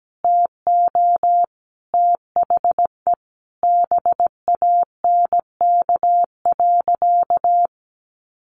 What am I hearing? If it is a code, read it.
Morse code, heard as a keyed tone, about 17 wpm, 700 Hz. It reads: TO THE BANK.
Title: Sentences from Top 1300 Words 17wpm